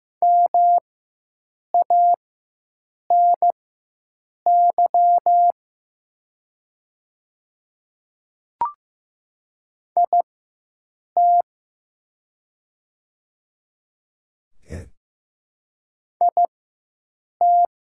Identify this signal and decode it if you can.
Morse code, heard as a keyed tone, about 15 wpm, spaced as for 7 wpm, 700 Hz. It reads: MANY IT IT
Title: 4x Character Spacing - Top 200 Words - 15wpm